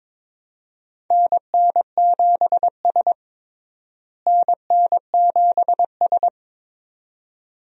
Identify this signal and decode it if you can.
Morse code, heard as a keyed tone, about 22 wpm, 700 Hz. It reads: NN7S NN7S
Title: USA Call Signs - 4 Characters - Repeated 2x - 22wpm